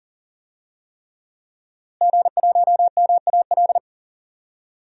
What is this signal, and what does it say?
Morse code, heard as a keyed tone, about 40 wpm, 700 Hz. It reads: G1MAL
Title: DXCC-100 Call Signs - 5 Characters - Repeated 3x - 40wpm